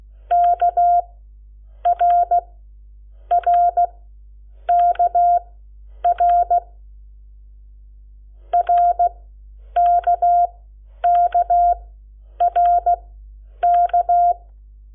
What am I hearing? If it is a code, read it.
Morse code, heard as a keyed tone, about 16 wpm, spaced as for 8 wpm, 700 Hz. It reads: KRRKR RKKRK
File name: Lecciones / CLASE Nº7